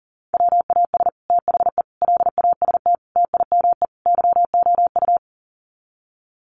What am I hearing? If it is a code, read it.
Morse code, heard as a keyed tone, about 40 wpm, 700 Hz. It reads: WAS THE LAST TIME YOU
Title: Sets of 5 Words Encore2 40wpm